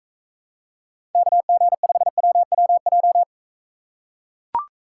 Morse code, heard as a keyed tone, about 40 wpm, 700 Hz. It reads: KG5WWJ